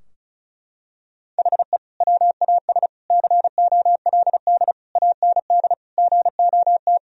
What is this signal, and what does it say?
Morse code, heard as a keyed tone, about 36 wpm, 700 Hz. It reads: HE WAS COLD AND GOT
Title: Sentences from Top 1000 Words - Review Part 4 - 35wpm